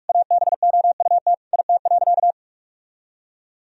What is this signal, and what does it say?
Morse code, heard as a keyed tone, about 46 wpm, 700 Hz. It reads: ABOUT IT.
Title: Sentences from Top 1000 Words - Review Part 4 - 45wpm